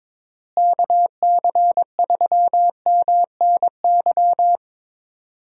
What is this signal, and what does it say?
Morse code, heard as a keyed tone, about 22 wpm, 700 Hz. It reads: KC3MNY
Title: USA Call Signs - 6 Characters - Repeated 2x - 22wpm